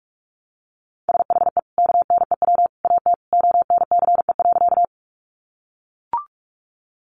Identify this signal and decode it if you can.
Morse code, heard as a keyed tone, about 44 wpm, 700 Hz. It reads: SHE KNEW AT ONCE.
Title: Sentences from Top 400 Words 45wpm